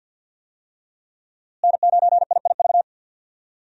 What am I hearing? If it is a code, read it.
Morse code, heard as a keyed tone, about 49 wpm, 700 Hz. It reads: N9IIV
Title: USA Call Signs - 5 Characters - Repeated 2x - 50wpm